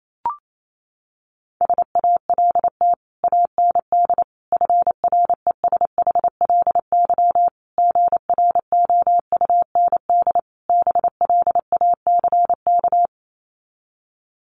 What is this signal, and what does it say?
Morse code, heard as a keyed tone, about 28 wpm, 700 Hz. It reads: SALT AND FRESHLY GROUND BLACK
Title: Sets of 5 Words Encore 28wpm